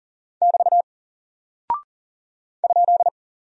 Morse code, heard as a keyed tone, about 40 wpm, 700 Hz. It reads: <BT> ?
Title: Single Character (Letter:Number:Punctuation:Prosign) Alt 40wpm